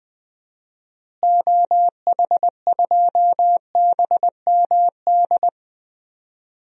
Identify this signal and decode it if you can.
Morse code, heard as a keyed tone, about 20 wpm, 700 Hz. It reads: OH2BMD
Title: DXCC-100 Call Signs - 6 Characters - Repeated 3x - 20wpm